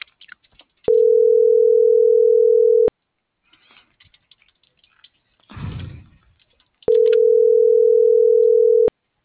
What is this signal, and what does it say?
Ringback tone (North America), 440 + 480 Hz.